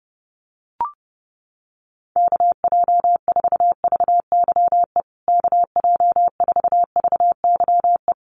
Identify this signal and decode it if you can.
Morse code, heard as a keyed tone, about 30 wpm, 700 Hz. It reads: KJ4VYE KJ4VYE
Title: USA Call Signs Repeated 2x 30wpm